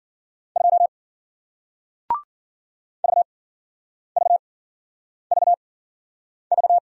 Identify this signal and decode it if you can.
Morse code, heard as a keyed tone, about 55 wpm, spaced as for 9 wpm, 700 Hz. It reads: 2 VVVV